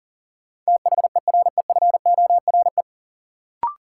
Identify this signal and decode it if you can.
Morse code, heard as a keyed tone, about 40 wpm, 700 Hz. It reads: THEREFORE
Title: Sentences from Top 800 Words - New Word Drill - 40wpm